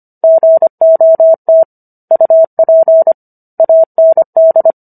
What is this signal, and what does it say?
Morse code, heard as a keyed tone, about 25 wpm, 650 Hz. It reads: GOT UP AND